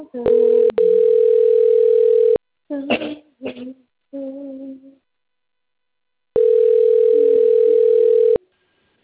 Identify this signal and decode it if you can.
Ringback tone (North America), 440 + 480 Hz, with speech.
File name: rg-1600-unknown- - - .15218